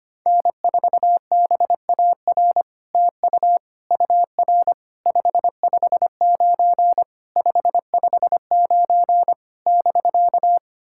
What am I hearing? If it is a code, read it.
Morse code, heard as a keyed tone, about 25 wpm, 700 Hz. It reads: N4BAR TU UR 559 559 <BK>
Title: SOTA QSO Practice 25wpm